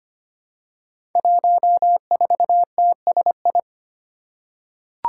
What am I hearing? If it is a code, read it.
Morse code, heard as a keyed tone, about 25 wpm, 700 Hz. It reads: 14TSI
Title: Five Challenging Letter-Number Combinations 25wpm